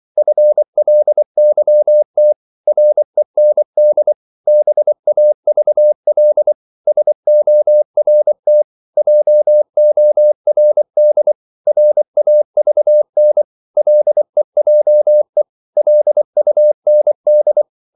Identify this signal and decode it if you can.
Morse code, heard as a keyed tone, about 24 wpm, 600 Hz. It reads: FLYT REND BAVL SORT JORD RAVN LEJE LUND
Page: Korte danske ord 24wpm